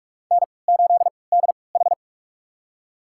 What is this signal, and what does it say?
Morse code, heard as a keyed tone, about 44 wpm, spaced as for 25 wpm, 700 Hz. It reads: N8DH
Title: USA Call Signs - 4 Characters - 3x Character Spacing - 45wpm